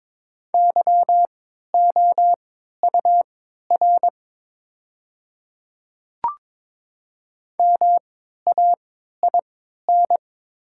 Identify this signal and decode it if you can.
Morse code, heard as a keyed tone, about 22 wpm, spaced as for 13 wpm, 690 Hz. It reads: YOUR MAIN
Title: 3x Character Spacing - Top 300 Words - 22wpm